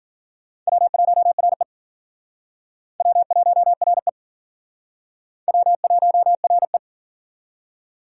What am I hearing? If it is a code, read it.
Morse code, heard as a keyed tone, about 52 wpm, 700 Hz. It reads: W1RE W1RE W1RE